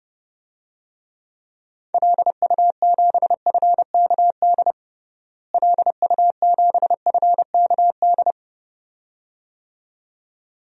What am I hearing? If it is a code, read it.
Morse code, heard as a keyed tone, about 31 wpm, 700 Hz. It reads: LU7FKD LU7FKD
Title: DXCC-100 Call Signs - 6 Characters - Repeated 2x - 30wpm